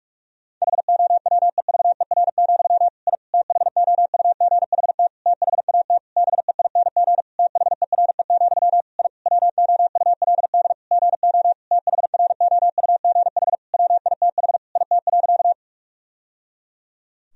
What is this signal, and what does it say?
Morse code, heard as a keyed tone, about 46 wpm, 700 Hz. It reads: HOWEVER, I THOUGHT THAT BEING THERE, I WOULD GO THROUGH WITH IT.
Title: Sentences from Top 1000 Words - Review Part 3 - 45wpm